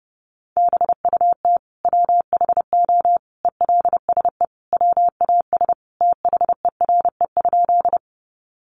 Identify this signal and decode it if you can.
Morse code, heard as a keyed tone, about 30 wpm, 700 Hz. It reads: BUT WHO ELSE WAS THERE?
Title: Sentences from Top 800 Words 30wpm